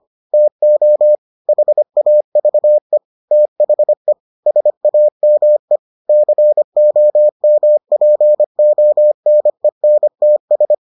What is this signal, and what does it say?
Morse code, heard as a keyed tone, about 25 wpm, 600 Hz. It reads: TO HAVE THE SAME COMPONENTS